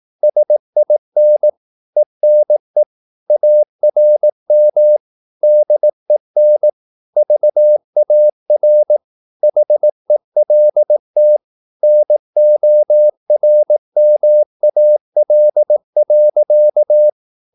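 Morse code, heard as a keyed tone, about 18 wpm, 600 Hz. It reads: SIN ENE ARM DEN VAR HELT NORMAL.
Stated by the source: Slow (QRS) Morse practice at 18 wpm, CW med Gnister.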